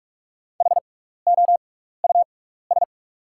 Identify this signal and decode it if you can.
Morse code, heard as a keyed tone, about 44 wpm, spaced as for 15 wpm, 700 Hz. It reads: HOUS
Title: 6x Character Spacing - Top 300 Words - 45wpm